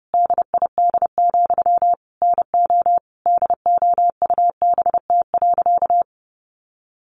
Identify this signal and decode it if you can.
Morse code, heard as a keyed tone, about 30 wpm, 700 Hz. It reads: DID, NO DOUBT.
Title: Sentences from Top 1500 Words 30wpm